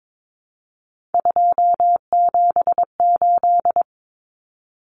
Morse code, heard as a keyed tone, about 22 wpm, 700 Hz. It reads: 278
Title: Common Cut Numbers - 3-Digit 22wpm